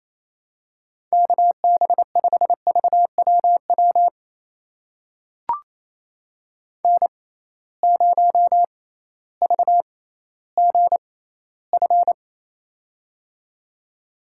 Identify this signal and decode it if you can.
Morse code, heard as a keyed tone, about 27 wpm, spaced as for 15 wpm, 700 Hz. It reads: KB5VWW N0VGF
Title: USA Call Signs - 6x Character Spacing - 28wpm